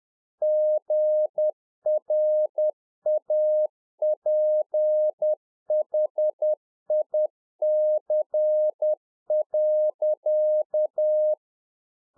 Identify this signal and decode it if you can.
Morse code, heard as a keyed tone, about 10 wpm, 610 Hz. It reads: GRAPHIC.